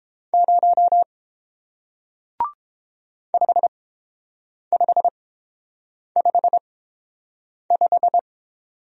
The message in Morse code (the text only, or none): 0 5555